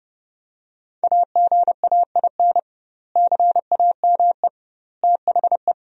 Morse code, heard as a keyed tone, about 30 wpm, 700 Hz. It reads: AGAIN CAME THE
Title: 2x Word-Spacing - Sentences from Top 500 Words - 30wpm